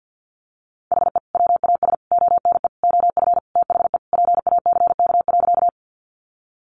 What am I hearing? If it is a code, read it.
Morse code, heard as a keyed tone, about 50 wpm, 700 Hz. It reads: HE WAS ONE OF THE PACK.
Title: Sentences from Top 1300 Words 50wpm